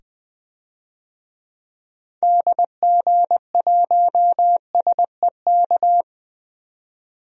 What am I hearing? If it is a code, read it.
Morse code, heard as a keyed tone, about 20 wpm, 700 Hz. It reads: DG1SEK
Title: DXCC-100 Call Signs - 6 Characters - 20wpm